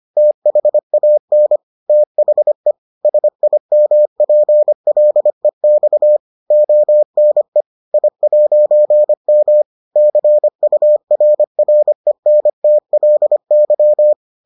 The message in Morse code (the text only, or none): THAN THE SIMPLEX ONE I'M CURRENTLY